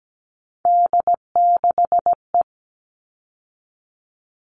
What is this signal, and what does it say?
Morse code, heard as a keyed tone, about 17 wpm, 700 Hz. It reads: D6E
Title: All Cut Numbers - 3-Digit 17wpm